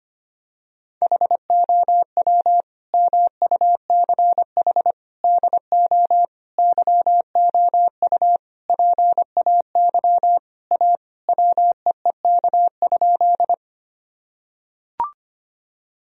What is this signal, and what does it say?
Morse code, heard as a keyed tone, about 25 wpm, 700 Hz. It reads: HOW MUCH DO YOU PAY A WEEK?
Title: Sentences from Top 1000 Words - Review Part 2 - 25wpm